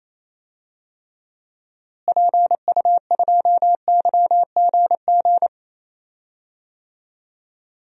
Morse code, heard as a keyed tone, about 28 wpm, 700 Hz. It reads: PU2YGG
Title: DXCC-100 Call Signs - 6 Characters - Repeated 3x - 28wpm